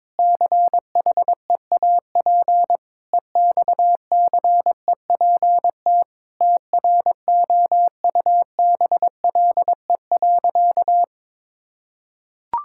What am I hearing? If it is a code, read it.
Morse code, heard as a keyed tone, about 22 wpm, 700 Hz. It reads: CHEAP EXCEPT TROUBLE.
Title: Sentences from Top 1600 Words 22wpm